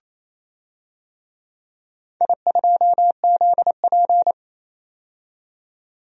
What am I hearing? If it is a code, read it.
Morse code, heard as a keyed tone, about 28 wpm, 700 Hz. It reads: I2ZP
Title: DXCC-100 Call Signs - 4 Characters - 28wpm